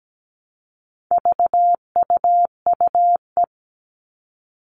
Morse code, heard as a keyed tone, about 17 wpm, 700 Hz. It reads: VUUE
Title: All Cut Numbers - 4-Digit 17wpm